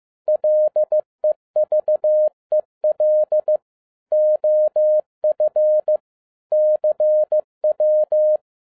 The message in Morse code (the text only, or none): LEVEL OF CW